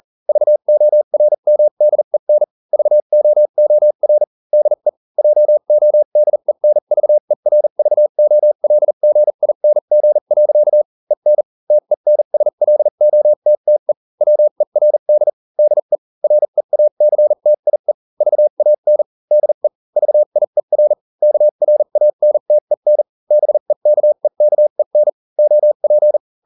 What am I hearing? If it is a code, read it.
Morse code, heard as a keyed tone, about 40 wpm, spaced as for 35 wpm, 600 Hz. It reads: VORMDEN VOOR DE JODENVERVOLGING. EN TENSLOTTE WERD DE REACTIE VAN DE VIER KRANTEN BEKEKEN OP